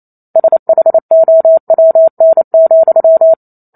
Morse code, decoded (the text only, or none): SHOWN,